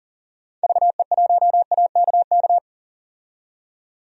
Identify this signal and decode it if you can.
Morse code, heard as a keyed tone, about 40 wpm, 700 Hz. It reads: VE1AKK
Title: DXCC-100 Call Signs - 6 Characters - 40wpm